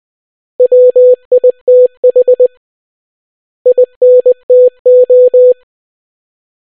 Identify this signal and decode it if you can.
Morse code, heard as a keyed tone, about 20 wpm, 500 Hz. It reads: WITH INTO